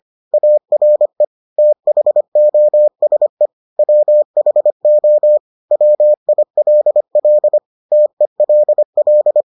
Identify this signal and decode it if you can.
Morse code, heard as a keyed tone, about 25 wpm, 600 Hz. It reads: ARE THOSE WHO WILL TELL